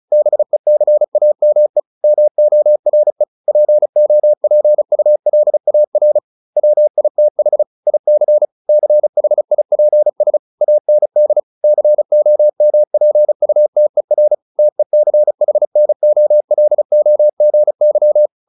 Morse code, heard as a keyed tone, about 34 wpm, 600 Hz. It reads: BECAME MORE POPULAR WITH IC CHIPS AND COMPUTER TECHNOLOGY